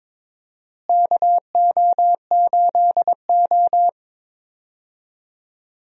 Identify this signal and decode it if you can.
Morse code, heard as a keyed tone, about 22 wpm, 700 Hz. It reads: KO8O
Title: USA Call Signs - 4 Characters - Repeated 2x - 22wpm